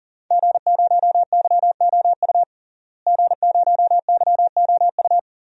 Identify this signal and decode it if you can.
Morse code, heard as a keyed tone, about 40 wpm, 700 Hz. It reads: G0YOU G0YOU
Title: DXCC-100 Call Signs - 5 Characters - Repeated 2x - 40wpm